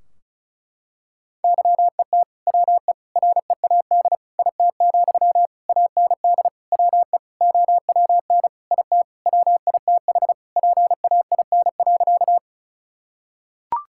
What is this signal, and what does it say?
Morse code, heard as a keyed tone, about 34 wpm, 700 Hz. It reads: YET WE READ IT, AND WE OWN IT WITH PAIN.
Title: Sentences from Top 1600 Words 35wpm